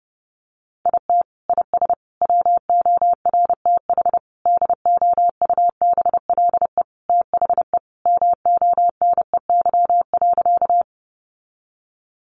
Morse code, heard as a keyed tone, about 30 wpm, 700 Hz. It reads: IT IS WORTH DOUBLE THE MONEY.
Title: Sentences from Top 1400 Words 30wpm